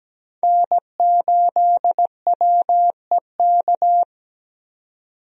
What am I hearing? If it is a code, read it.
Morse code, heard as a keyed tone, about 17 wpm, 700 Hz. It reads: N8WEK
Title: USA Call Signs - 5 Characters - 17wpm